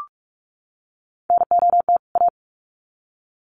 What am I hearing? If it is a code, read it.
Morse code, heard as a keyed tone, about 44 wpm, 700 Hz. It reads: NOT A
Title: Sets of 2 Words 45wpm